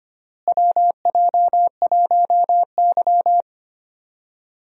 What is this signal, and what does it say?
Morse code, heard as a keyed tone, about 25 wpm, 700 Hz. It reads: WJ1Y
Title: USA Call Signs - 4 Characters - Repeated 3x - 25wpm